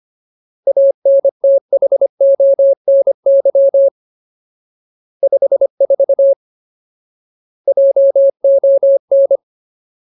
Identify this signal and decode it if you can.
Morse code, heard as a keyed tone, about 25 wpm, 550 Hz. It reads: ANTHONY 54 JON
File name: Ultimate Word List / 25WPM Ultimate